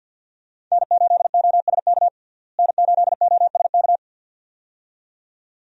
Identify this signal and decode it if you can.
Morse code, heard as a keyed tone, about 51 wpm, 700 Hz. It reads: N8OSK N8OSK
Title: USA Call Signs - 5 Characters - Repeated 2x - 50wpm